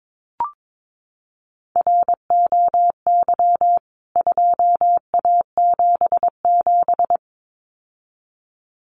Morse code, heard as a keyed tone, about 22 wpm, 700 Hz. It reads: ROY 2A77
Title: CWT Exchange 22wpm